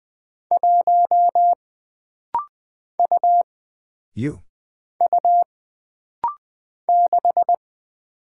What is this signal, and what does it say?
Morse code, heard as a keyed tone, about 20 wpm, 700 Hz. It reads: 1 U U 6
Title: Single Character (Letter:Number:Punctuation:Prosign) - Rapid-Fire - 20wpm